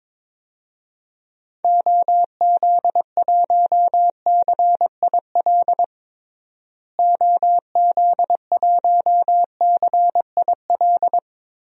Morse code, heard as a keyed tone, about 22 wpm, 700 Hz. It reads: OZ1CIL OZ1CIL
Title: DXCC-100 Call Signs - 6 Characters - Repeated 2x - 22wpm